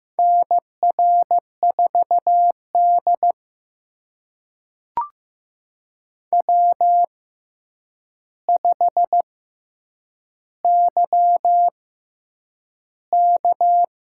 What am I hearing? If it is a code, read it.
Morse code, heard as a keyed tone, about 15 wpm, spaced as for 8 wpm, 700 Hz. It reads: NR4D W5YK
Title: USA Call Signs - 4 Characters - 6x Character Spacing - 15wpm